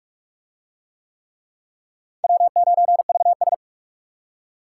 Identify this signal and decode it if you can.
Morse code, heard as a keyed tone, about 37 wpm, 700 Hz. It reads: W9VS
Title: Speed Racing - USA 1x2 Call Signs 30wpm